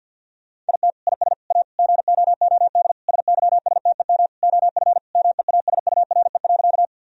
Morse code, heard as a keyed tone, about 51 wpm, 700 Hz. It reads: IT IS A GOOD SYSTEM OF MEASURE.
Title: Sentences from Top 1000 Words - Review Part 3 - 50wpm